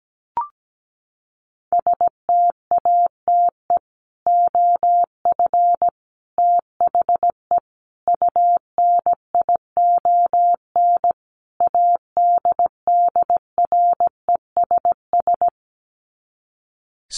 Morse code, heard as a keyed tone, about 17 wpm, 700 Hz. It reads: STATE OF THE UNION ADDRESS
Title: Sets of 5 Words Encore 17wpm